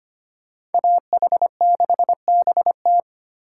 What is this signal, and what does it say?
Morse code, heard as a keyed tone, about 32 wpm, 700 Hz. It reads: AH6BT